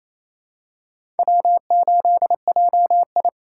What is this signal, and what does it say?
Morse code, heard as a keyed tone, about 27 wpm, 700 Hz. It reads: W8JI
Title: USA Call Signs - 2x Character Spacing - 28wpm